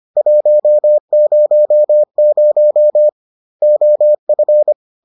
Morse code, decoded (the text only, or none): 100 OF